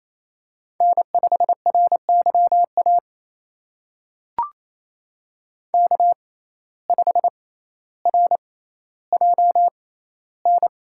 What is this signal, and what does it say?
Morse code, heard as a keyed tone, about 28 wpm, spaced as for 14 wpm, 700 Hz. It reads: N5RYA K5RJN
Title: USA Call Signs - 5 Characters - 6x Character Spacing - 28wpm